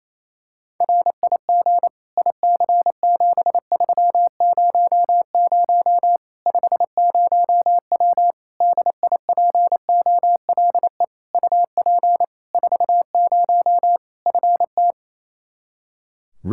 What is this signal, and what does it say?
Morse code, heard as a keyed tone, about 28 wpm, 700 Hz. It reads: RIG IC7300 50W DIPOLE UP 40 FT
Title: Practice QSO - Exchange 2 - Info - 28wpm